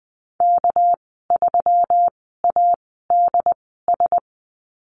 Morse code, heard as a keyed tone, about 20 wpm, spaced as for 14 wpm, 700 Hz. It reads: K3ADS
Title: 219 Lesson 35g - Number 0 - Call Signs 20wpm